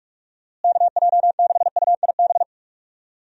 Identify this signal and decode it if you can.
Morse code, heard as a keyed tone, about 45 wpm, 700 Hz. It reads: KJ6UIB
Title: USA Call Signs - 6 Characters - Repeated 3x - 45wpm